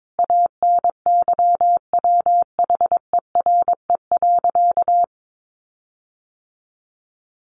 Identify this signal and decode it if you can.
Morse code, heard as a keyed tone, about 22 wpm, 700 Hz. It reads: ANYWHERE.
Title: Sentences from Top 1400 Words 22wpm